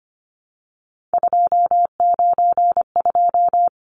Morse code, heard as a keyed tone, about 25 wpm, 700 Hz. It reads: 292
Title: Numbers - 3-Digit 25wpm